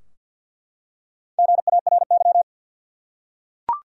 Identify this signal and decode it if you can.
Morse code, heard as a keyed tone, about 47 wpm, 700 Hz. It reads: GARY
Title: USA Common Names - Max 5 Letters - 50wpm